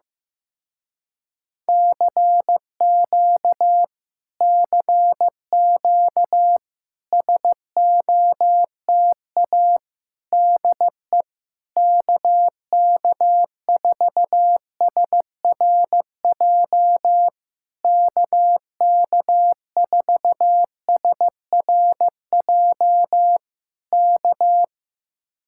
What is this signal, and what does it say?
Morse code, heard as a keyed tone, about 15 wpm, 700 Hz. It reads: CQ CQ SOTA DE KK4SRJ KK4SRJ K
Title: Speed-Racing Common QSO Phrases Encore - 15wpm